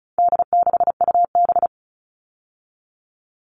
Morse code, heard as a keyed tone, about 34 wpm, 700 Hz. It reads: D6UB
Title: All Cut Numbers - 4-Digit 35wpm